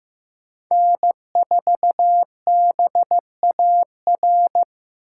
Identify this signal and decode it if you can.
Morse code, heard as a keyed tone, about 15 wpm, 700 Hz. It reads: N4BAR
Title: USA Call Signs - 5 Characters - 4x Character Spacing - 15wpm